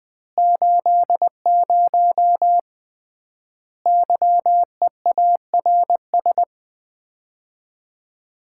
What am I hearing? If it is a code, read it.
Morse code, heard as a keyed tone, about 21 wpm, 700 Hz. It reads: 80 YEARS
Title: Common QSO Phrases - 3x Word-Spacing - 20wpm